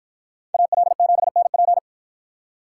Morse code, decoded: AL7NP

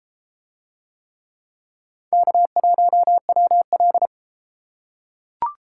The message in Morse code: K1WL

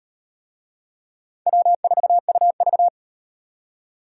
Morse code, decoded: W4UV

